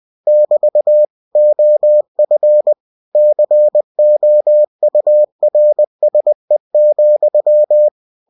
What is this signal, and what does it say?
Morse code, heard as a keyed tone, about 20 wpm, 600 Hz. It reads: <BT> OF COURSE,
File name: cw-books / winnie-the-pooh / winnie-20wpm-page-0054